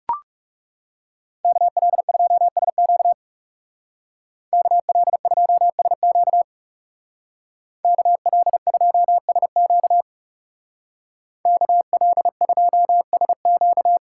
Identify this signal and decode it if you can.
Morse code, heard as a keyed tone, about 36 wpm, 700 Hz. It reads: KL2SQ KL2SQ KL2SQ KL2SQ
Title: Speed Racing - USA 2x2 Call Signs 30wpm